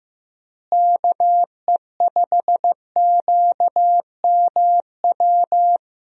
Morse code, heard as a keyed tone, about 15 wpm, 700 Hz. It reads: KE5QMW
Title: USA Call Signs - 6 Characters - Repeated 3x - 15wpm